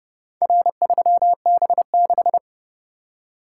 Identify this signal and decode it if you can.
Morse code, heard as a keyed tone, about 29 wpm, 700 Hz. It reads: R3B6